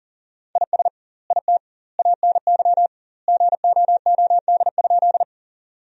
Morse code, heard as a keyed tone, about 37 wpm, 700 Hz. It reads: IS IT ANY GOOD?
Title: 2x Word-Spacing - Sentences from Top 200 Words - 40wpm